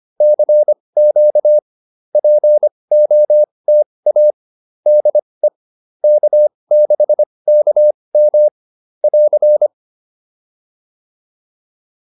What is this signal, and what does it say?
Morse code, heard as a keyed tone, about 25 wpm, spaced as for 20 wpm, 560 Hz. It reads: CQ POTA DE K6KM <AR>